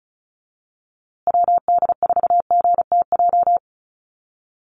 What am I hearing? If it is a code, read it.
Morse code, heard as a keyed tone, about 34 wpm, 700 Hz. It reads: WD4GTJ
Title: USA Call Signs Repeated 2x 35wpm